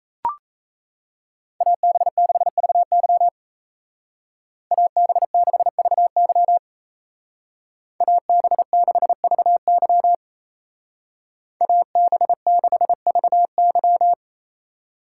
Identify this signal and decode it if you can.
Morse code, heard as a keyed tone, about 34 wpm, 700 Hz. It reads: AB6VY AB6VY AB6VY AB6VY
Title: Speed Racing - USA 2x2 Call Signs 28wpm